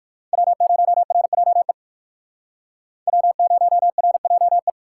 Morse code, heard as a keyed tone, about 66 wpm, spaced as for 57 wpm, 700 Hz. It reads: W0RJE W0RJE